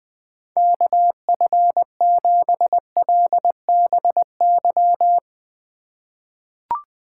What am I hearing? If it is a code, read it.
Morse code, heard as a keyed tone, about 20 wpm, 700 Hz. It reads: KF7LBY
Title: USA Call Signs - 4x Character Spacing - 20wpm